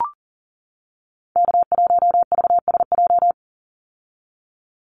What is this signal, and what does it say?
Morse code, heard as a keyed tone, about 40 wpm, 700 Hz. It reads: K1VSJ
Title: Call Signs - 40wpm